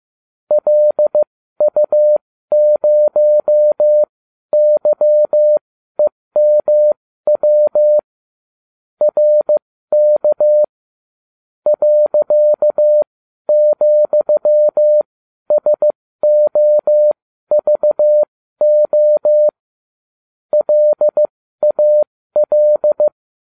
Morse code, heard as a keyed tone, about 15 wpm, spaced as for 12 wpm, 600 Hz. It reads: LU0YEMW RK .,SOVO LAL